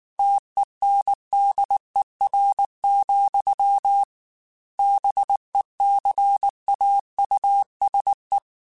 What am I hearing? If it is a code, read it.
Morse code, heard as a keyed tone, about 19 wpm, 800 Hz. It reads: TENDER, BECAUSE
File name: Chuckles / Chuckles-mp3s / SoftWords @18